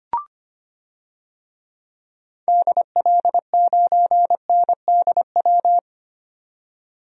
Morse code, heard as a keyed tone, about 25 wpm, 700 Hz. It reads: DL9NDW
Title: DXCC-100 Call Signs - 6 Characters - 25wpm